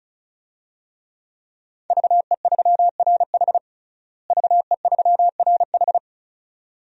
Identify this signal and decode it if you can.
Morse code, heard as a keyed tone, about 36 wpm, 700 Hz. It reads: VE3RH VE3RH
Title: DXCC-100 Call Signs - 5 Characters - Repeated 2x - 35wpm